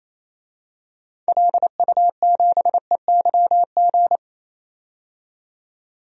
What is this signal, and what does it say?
Morse code, heard as a keyed tone, about 28 wpm, 700 Hz. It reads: LU7EYG
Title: DXCC-100 Call Signs - 6 Characters - 28wpm